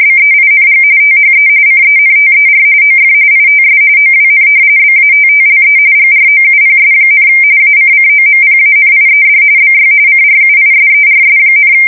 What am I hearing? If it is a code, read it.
RTTY, about 45.45 baud. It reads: BEING THE CENTER OF ITS OWN BATTLE GROUPS, ACCOMPANIED BY CRUISERS,
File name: fsk170 / battleshipnj